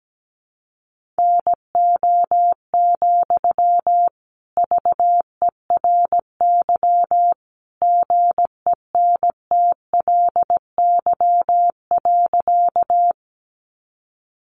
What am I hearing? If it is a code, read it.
Morse code, heard as a keyed tone, about 17 wpm, 700 Hz. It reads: NO, VERY GENTLY.
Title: Sentences from Top 1500 Words 17wpm